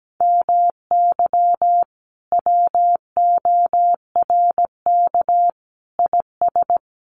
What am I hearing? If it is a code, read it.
Morse code, heard as a keyed tone, about 17 wpm, 700 Hz. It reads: MY WORK IS
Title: Sentences from Top 600 Words 17wpm